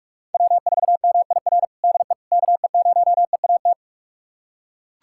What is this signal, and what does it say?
Morse code, heard as a keyed tone, about 41 wpm, 700 Hz. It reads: W4MIR DE KE0EAT
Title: USA Callsign DE Callsign 45wpm